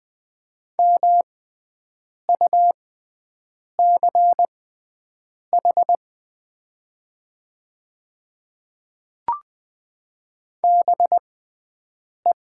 Morse code, heard as a keyed tone, about 20 wpm, spaced as for 7 wpm, 700 Hz. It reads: MUCH BE
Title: 6x Character Spacing - Top 200 Words - 20wpm